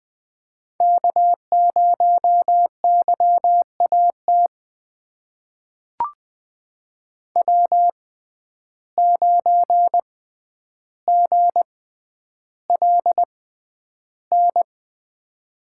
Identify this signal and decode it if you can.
Morse code, heard as a keyed tone, about 20 wpm, spaced as for 10 wpm, 700 Hz. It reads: K0YAT W9GLN
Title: USA Call Signs - 5 Characters - 6x Character Spacing - 20wpm